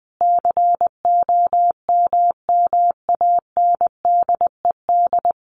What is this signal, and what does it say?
Morse code, heard as a keyed tone, about 20 wpm, 700 Hz. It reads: COMMANDED
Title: A Sound Of Thunder 20wpm